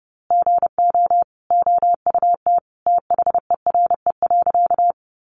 Morse code, heard as a keyed tone, about 30 wpm, 700 Hz. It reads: GO OUT THERE.
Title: A Sound Of Thunder 30wpm